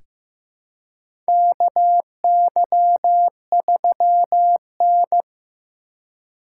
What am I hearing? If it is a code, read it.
Morse code, heard as a keyed tone, about 15 wpm, 700 Hz. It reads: KY3N
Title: USA Call Signs - 4 Characters - 3x Character Spacing - 15wpm